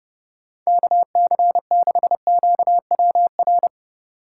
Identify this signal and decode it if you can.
Morse code, heard as a keyed tone, about 31 wpm, 700 Hz. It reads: KC6QWR